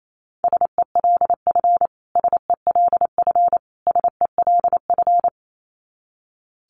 Morse code, heard as a keyed tone, about 28 wpm, 700 Hz. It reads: SELF SELF SELF
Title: Top 500 Words Repeated 28wpm